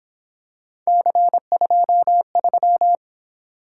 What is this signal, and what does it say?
Morse code, heard as a keyed tone, about 27 wpm, 700 Hz. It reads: C23
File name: Speed-Racing - Three Letter-Number Combinations - 22wpm